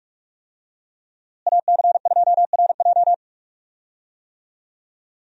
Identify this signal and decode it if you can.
Morse code, heard as a keyed tone, about 43 wpm, 700 Hz. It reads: AC2RJ
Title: USA Call Signs - 5 Characters - Repeated 3x - 45wpm